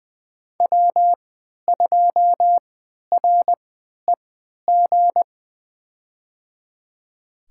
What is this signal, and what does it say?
Morse code, heard as a keyed tone, about 20 wpm, spaced as for 11 wpm, 700 Hz. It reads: W2REG
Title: USA Call Signs - 3x Character Spacing - 20wpm